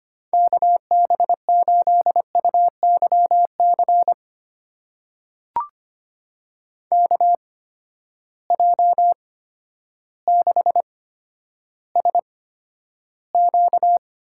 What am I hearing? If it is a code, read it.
Morse code, heard as a keyed tone, about 25 wpm, spaced as for 11 wpm, 700 Hz. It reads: KB8UYC KJ6SQ
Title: USA Call Signs - 6 Characters - 8x Character Spacing - 25wpm